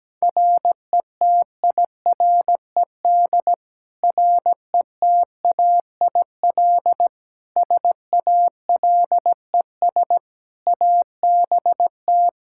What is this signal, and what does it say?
Morse code, heard as a keyed tone, about 17 wpm, 700 Hz. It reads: RETIRED RETAIL SALES ABT
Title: Practice QSO - Exchange 3 - More Info - 17wpm